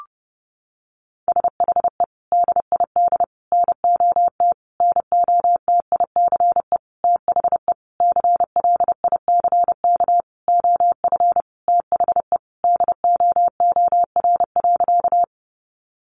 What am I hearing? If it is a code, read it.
Morse code, heard as a keyed tone, about 30 wpm, 700 Hz. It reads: SHE DID NOT NOTICE THE CLICK OF THE DOOR.
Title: Sentences from Top 1100 Words 30wpm